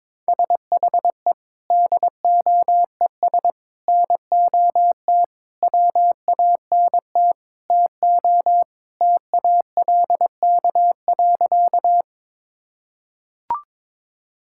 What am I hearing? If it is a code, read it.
Morse code, heard as a keyed tone, about 22 wpm, 700 Hz. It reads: SHE DOES NOT WANT TO TALK.
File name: practice / 22wpm / Sentences with 4-Word Phrases - 22wpm